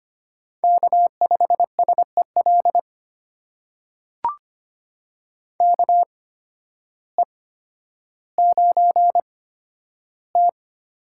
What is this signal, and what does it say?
Morse code, heard as a keyed tone, about 25 wpm, spaced as for 12 wpm, 700 Hz. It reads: K5SEL KE9T
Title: USA Call Signs - 5 Characters - 8x Character Spacing - 25wpm